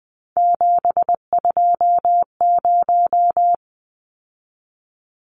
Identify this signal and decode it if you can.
Morse code, heard as a keyed tone, about 20 wpm, 700 Hz. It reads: 720
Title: Numbers - 3-Digit 20wpm